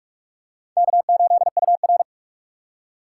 Morse code, heard as a keyed tone, about 45 wpm, 700 Hz. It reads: K8UR
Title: USA Call Signs - 4 Characters - 4x Character Spacing - 45wpm